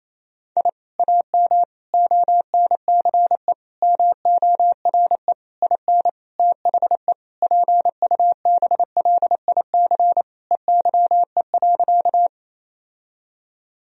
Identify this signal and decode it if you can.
Morse code, heard as a keyed tone, about 28 wpm, 700 Hz. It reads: I AM ONCE MORE IN THE PUBLIC EYE.
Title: Sentences from Top 1000 Words - Review Part 3 - 28wpm